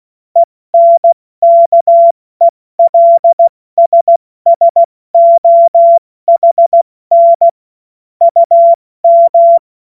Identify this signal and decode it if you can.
Morse code, heard as a keyed tone, about 16 wpm, 680 Hz. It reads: ENKELSSOHN UM
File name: Morsekurs / Witze / Witz 17. August 2021, 70 F80